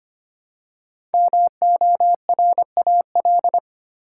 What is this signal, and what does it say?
Morse code, heard as a keyed tone, about 25 wpm, 700 Hz. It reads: MORAL
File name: 5 Letter Words - First 3 Letters Given - 25wpm